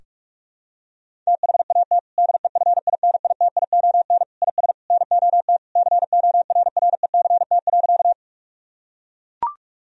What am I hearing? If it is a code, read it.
Morse code, heard as a keyed tone, about 41 wpm, 700 Hz. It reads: THAT DEFINITION IS NOT CORRECT.
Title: Sentences from Top 1000 Words - Review Part 4 - 45wpm